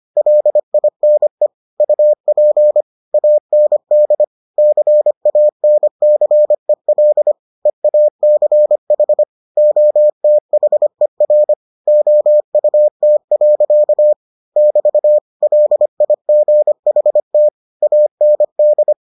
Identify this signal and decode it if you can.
Morse code, heard as a keyed tone, about 25 wpm, 600 Hz. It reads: LINE UP AND CANCEL EACH OTHER OUT. <BT> LIGHT AND